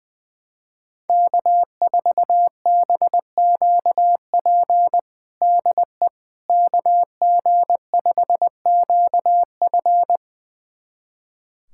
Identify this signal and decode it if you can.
Morse code, heard as a keyed tone, about 21 wpm, 700 Hz. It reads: K4BQP DE KG5QF
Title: USA Callsign DE Callsign 20wpm